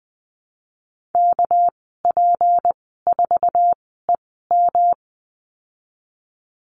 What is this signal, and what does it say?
Morse code, heard as a keyed tone, about 20 wpm, spaced as for 15 wpm, 700 Hz. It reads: KP4EM
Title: 200 Lesson 32g - Letter k - Call Signs 20wpm